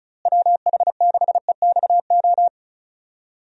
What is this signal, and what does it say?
Morse code, heard as a keyed tone, about 34 wpm, 700 Hz. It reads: WH6EXO
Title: USA Call Signs - 6 Characters - Repeated 3x - 35wpm